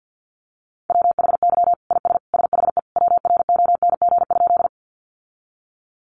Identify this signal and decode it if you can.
Morse code, heard as a keyed tone, about 50 wpm, 700 Hz. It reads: WHY IS SHE WRONG?
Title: Sentences from Top 900 Words 50wpm